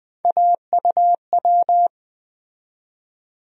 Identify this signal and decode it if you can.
Morse code, heard as a keyed tone, about 20 wpm, 700 Hz. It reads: AUW
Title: USA Call Sign Postfixes - 3 Letters 20wpm